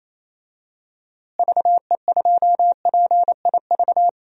Morse code, heard as a keyed tone, about 28 wpm, 700 Hz. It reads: VE2PIV
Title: DXCC-100 Call Signs - 6 Characters - Repeated 3x - 28wpm